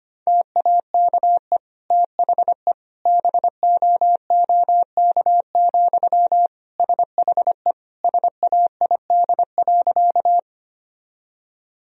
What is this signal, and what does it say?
Morse code, heard as a keyed tone, about 25 wpm, 700 Hz. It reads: TAKE THE BOOK, SHE SAID.
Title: Sentences from Top 1000 Words - Review Part 1 - 25wpm